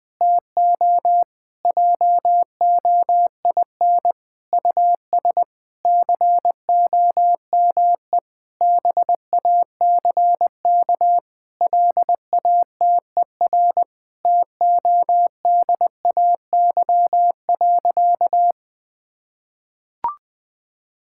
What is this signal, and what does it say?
Morse code, heard as a keyed tone, about 20 wpm, 700 Hz. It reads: TO JOIN US COME BACK LATER TODAY.
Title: Sentences with 4-Word Phrases - 20wpm